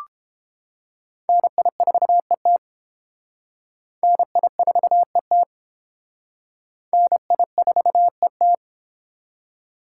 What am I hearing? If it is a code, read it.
Morse code, heard as a keyed tone, about 32 wpm, 700 Hz. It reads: NI4ET NI4ET NI4ET